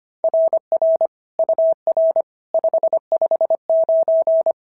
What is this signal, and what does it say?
Morse code, heard as a keyed tone, about 24 wpm, 650 Hz. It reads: RR UR 559
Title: SOTA QSO Practice 25wpm